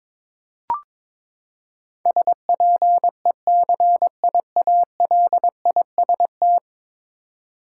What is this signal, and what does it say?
Morse code, heard as a keyed tone, about 22 wpm, 700 Hz. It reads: SPECIALIST
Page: morse-code-ninja-practice-22wpm